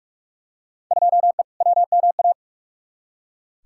Morse code, heard as a keyed tone, about 43 wpm, 700 Hz. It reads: 2E WMA
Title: Field Day Exchanges Encore - 45wpm